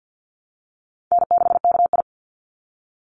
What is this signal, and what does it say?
Morse code, heard as a keyed tone, about 49 wpm, 700 Hz. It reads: N6KI
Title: Call Signs - 50wpm